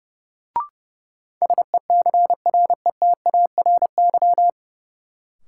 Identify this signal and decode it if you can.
Morse code, heard as a keyed tone, about 31 wpm, 700 Hz. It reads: SECRETARY
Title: Sentences from Top 1300 Words - New Word Drill Alt - 30wpm